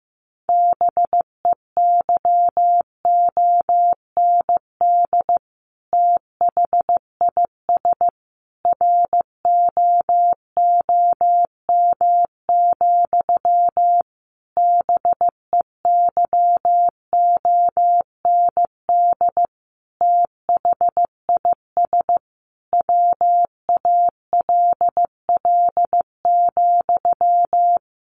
Morse code, heard as a keyed tone, about 15 wpm, 700 Hz. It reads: BEYOND THIS ROOM, BEYOND THIS WALL,
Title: A Sound Of Thunder 15wpm